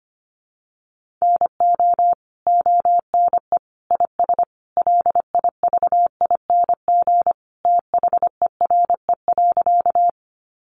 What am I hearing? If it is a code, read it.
Morse code, heard as a keyed tone, about 25 wpm, 700 Hz. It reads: NO ONE IS LIVING THERE.
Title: Sentences from Top 1100 Words 25wpm